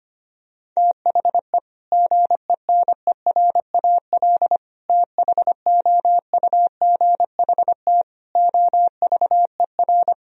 Morse code, heard as a keyed tone, about 25 wpm, 700 Hz. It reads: THE GENERAL THOUGHT OVER
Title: Sentences from Top 1000 Words - Review Part 3 - 25wpm